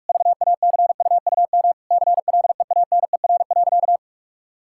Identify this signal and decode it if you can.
Morse code, heard as a keyed tone, about 44 wpm, 700 Hz. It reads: VACUUM CLEANER.
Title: Pangram Challenge v2 - 45wpm